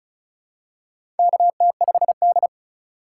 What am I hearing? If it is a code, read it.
Morse code, heard as a keyed tone, about 36 wpm, 690 Hz. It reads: KT5D
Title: USA Call Signs - 4 Characters - 2x Character Spacing - 35wpm